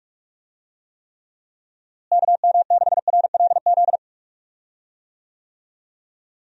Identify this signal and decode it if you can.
Morse code, heard as a keyed tone, about 45 wpm, 700 Hz. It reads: KM6RLZ
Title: USA Call Signs - 6 Characters - Repeated 3x - 45wpm